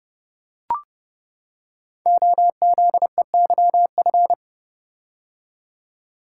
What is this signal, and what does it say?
Morse code, heard as a keyed tone, about 28 wpm, 700 Hz. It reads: OZEYF
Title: Code Groups - 5 Letter:Numbers - 30wpm